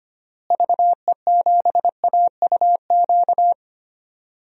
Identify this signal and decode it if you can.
Morse code, heard as a keyed tone, about 25 wpm, 700 Hz. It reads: VE7AUQ
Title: DXCC-100 Call Signs - 6 Characters - Repeated 3x - 25wpm